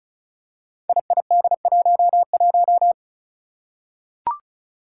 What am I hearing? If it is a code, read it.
Morse code, heard as a keyed tone, about 36 wpm, 700 Hz. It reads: IID11
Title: Five Challenging Letter-Number Combinations 35wpm